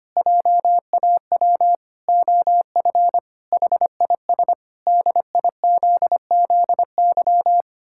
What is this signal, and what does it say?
Morse code, heard as a keyed tone, about 25 wpm, 700 Hz. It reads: JAW OF HIS DIZZY
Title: Pangram Challenge v2 - 25wpm